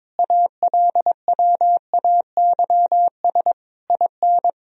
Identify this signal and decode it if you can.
Morse code, heard as a keyed tone, about 22 wpm, 700 Hz. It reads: ALWAYS IN
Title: Sentences from Top 1000 Words - Review Part 3 - 22wpm